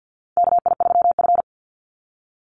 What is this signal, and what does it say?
Morse code, heard as a keyed tone, about 50 wpm, 700 Hz. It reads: KI3F
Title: Call Signs Repeated 2x - 50wpm